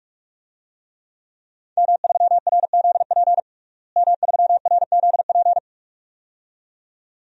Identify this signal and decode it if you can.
Morse code, heard as a keyed tone, about 46 wpm, 700 Hz. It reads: M3RZP M3RZP
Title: DXCC-100 Call Signs - 5 Characters - Repeated 2x - 45wpm